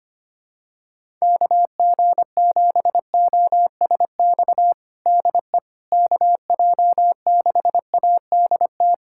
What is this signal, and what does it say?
Morse code, heard as a keyed tone, about 25 wpm, 700 Hz. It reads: KG7OSX DE KJ6ADT
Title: USA Callsign DE Callsign 25wpm